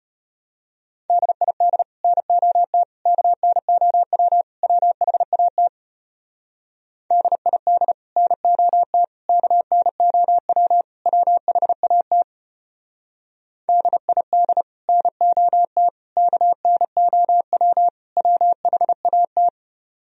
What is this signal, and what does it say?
Morse code, heard as a keyed tone, about 30 wpm, 700 Hz. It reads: DID NOT KNOW WHAT DID NOT KNOW WHAT DID NOT KNOW WHAT
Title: Speed-Racing - Sets of 4 Words 25wpm